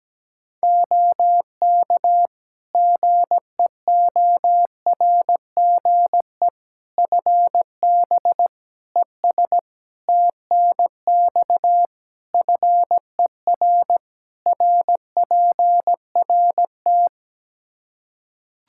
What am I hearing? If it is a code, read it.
Morse code, heard as a keyed tone, about 17 wpm, 700 Hz. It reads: OK GEORGE FB ES TNX FER RPRT
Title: K1USN SST QSO Practice 17wpm